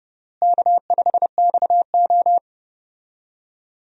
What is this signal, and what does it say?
Morse code, heard as a keyed tone, about 30 wpm, 700 Hz. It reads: K5XO
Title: USA Call Signs - 4 Characters - Repeated 3x - 30wpm